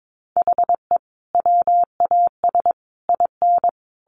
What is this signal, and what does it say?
Morse code, heard as a keyed tone, about 22 wpm, 700 Hz. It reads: HE WAS IN
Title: Sentences from Top 1300 Words 22wpm